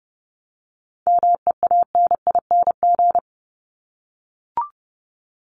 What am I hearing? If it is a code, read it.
Morse code, heard as a keyed tone, about 30 wpm, 700 Hz. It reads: MEANING
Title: Top 751-1000 Words 30wpm